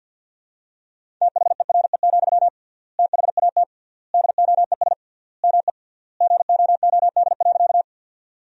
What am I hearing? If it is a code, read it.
Morse code, heard as a keyed tone, about 51 wpm, 700 Hz. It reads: THERE, THAT DOES ME GOOD.
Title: 3x Word-Spacing - Sentences from Top 200 Words - 50wpm